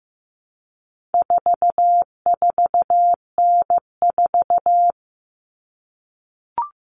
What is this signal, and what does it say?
Morse code, heard as a keyed tone, about 15 wpm, 700 Hz. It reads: 44N4
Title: All Cut Numbers - 4-Digit 15wpm